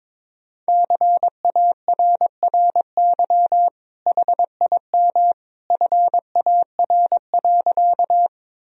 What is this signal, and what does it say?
Morse code, heard as a keyed tone, about 22 wpm, 700 Hz. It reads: CARRY HIM FAR.
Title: Sentences from Top 1000 Words - Review Part 1 - 22wpm